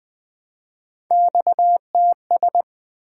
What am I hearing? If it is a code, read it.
Morse code, heard as a keyed tone, about 20 wpm, 700 Hz. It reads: XTS
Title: Three Letter-Number Combinations - 20wpm